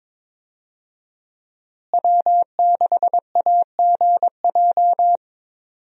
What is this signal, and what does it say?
Morse code, heard as a keyed tone, about 21 wpm, 700 Hz. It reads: W6AGJ
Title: USA Call Signs - 5 Characters - Repeated 3x - 22wpm